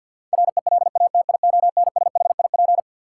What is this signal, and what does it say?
Morse code, heard as a keyed tone, about 48 wpm, 700 Hz. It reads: RELATIONSHIP
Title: Sentences from Top 1100 Words - New Word Drill Alt - 50wpm